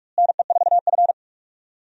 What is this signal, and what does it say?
Morse code, heard as a keyed tone, about 43 wpm, 700 Hz. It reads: NE4F
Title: USA Call Signs - 4 Characters - Repeated 3x - 45wpm